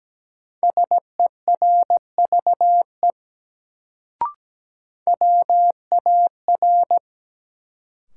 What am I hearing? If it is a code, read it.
Morse code, heard as a keyed tone, about 17 wpm, 700 Hz. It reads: SERVE WAR
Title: Sentences from Top 400 Words - New Word Drill - 17wpm